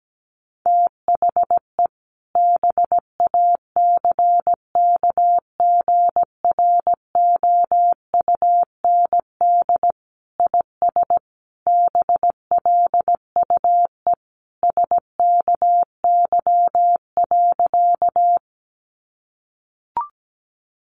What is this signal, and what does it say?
Morse code, heard as a keyed tone, about 17 wpm, 700 Hz. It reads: THE BACKGROUND IS BLUE SKY.
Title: Sentences from Top 1300 Words 17wpm